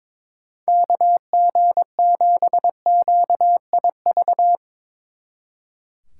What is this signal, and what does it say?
Morse code, heard as a keyed tone, about 23 wpm, 700 Hz. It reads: KG7QIV
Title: Speed Racing - USA 2x3 Call Signs 22wpm